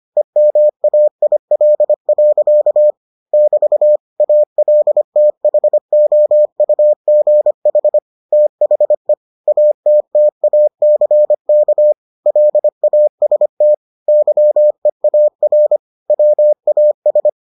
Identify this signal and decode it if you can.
Morse code, heard as a keyed tone, about 25 wpm, 600 Hz. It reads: EMAIL. <BT> ALTHOUGH THE ATTACK LAST YEAR WAS